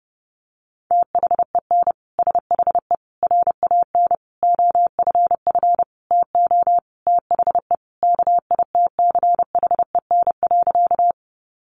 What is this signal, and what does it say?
Morse code, heard as a keyed tone, about 30 wpm, 700 Hz. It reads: THEN SHE RAN OFF TO THE KITCHEN.
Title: Sentences from Top 1400 Words 30wpm